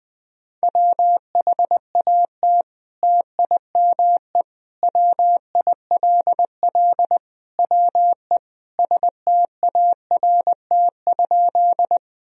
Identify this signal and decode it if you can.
Morse code, heard as a keyed tone, about 21 wpm, 700 Hz. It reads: WHAT TIME WILL WE START?
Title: Sentences from Top 1000 Words - Review Part 1 - 20wpm